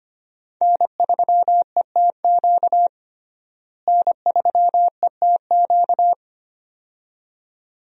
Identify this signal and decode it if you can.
Morse code, heard as a keyed tone, about 25 wpm, 700 Hz. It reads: N3ETQ N3ETQ
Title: USA Call Signs - 5 Characters - Repeated 2x - 25wpm